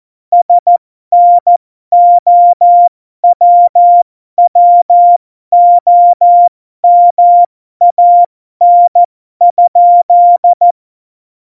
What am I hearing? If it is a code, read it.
Morse code, heard as a keyed tone, about 14 wpm, spaced as for 12 wpm, 700 Hz. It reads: SNOWWOMAN?
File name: Morsekurs / Witze